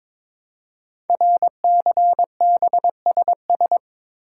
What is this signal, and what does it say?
Morse code, heard as a keyed tone, about 22 wpm, 700 Hz. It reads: RCBSS
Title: Code Groups - 5 Letters - 22wpm